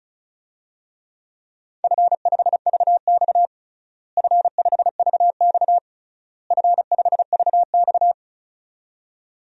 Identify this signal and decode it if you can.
Morse code, heard as a keyed tone, about 34 wpm, 700 Hz. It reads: F5VX F5VX F5VX
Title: DXCC-100 Call Signs - 4 Characters - Repeated 3x - 35wpm